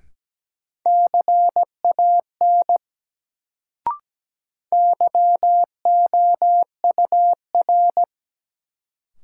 Morse code, heard as a keyed tone, about 17 wpm, 700 Hz. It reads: CAN YOUR
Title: Sets of 2 Words - New Word Drill - 17wpm